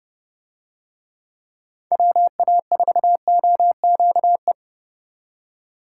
Morse code, heard as a keyed tone, about 31 wpm, 700 Hz. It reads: WA4OQE